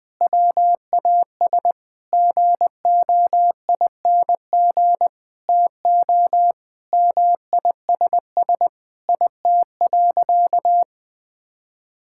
Morse code, heard as a keyed tone, about 21 wpm, 700 Hz. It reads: WAS GOING TO MISS IT.
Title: Sentences with 7-Word Phrases - 20wpm